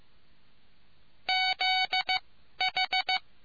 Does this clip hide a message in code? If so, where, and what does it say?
Yes: Morse code from 1.3 to 3.2 s, heard as a keyed tone, about 15 wpm, spaced as for 11 wpm, 750 Hz. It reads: ZH